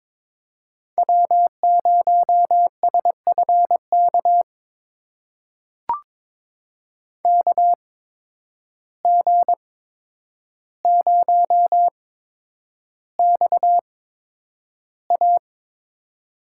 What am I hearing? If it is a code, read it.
Morse code, heard as a keyed tone, about 22 wpm, spaced as for 9 wpm, 700 Hz. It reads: W0SFK KG0XA
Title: USA Call Signs - 5 Characters - 8x Character Spacing - 22wpm